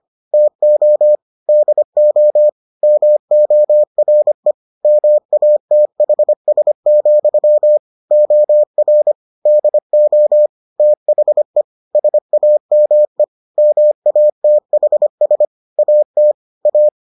Morse code, heard as a keyed tone, about 25 wpm, 600 Hz. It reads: TO DO MORE MATHS, OR DO THE SAME MATHS AT A